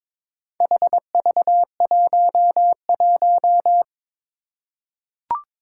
Morse code, heard as a keyed tone, about 22 wpm, 700 Hz. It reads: HV11
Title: Four Challenging Letter-Number Combinations 22wpm